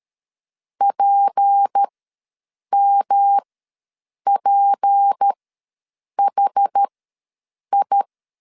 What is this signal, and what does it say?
Morse code, heard as a keyed tone, about 13 wpm, spaced as for 7 wpm, 800 Hz. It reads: PMPHI
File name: LEZIONI / 6 PKX0 / 6 P ws 12wpm 3117